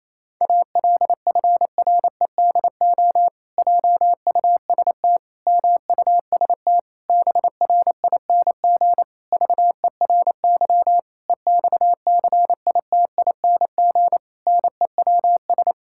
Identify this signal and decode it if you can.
Morse code, heard as a keyed tone, about 28 wpm, 700 Hz. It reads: ALFREDO JUST MUST BRING VERY EXCITING NEWS
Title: Pangram Challenge v2 - 28wpm